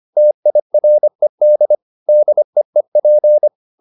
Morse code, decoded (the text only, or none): TIRED DEEP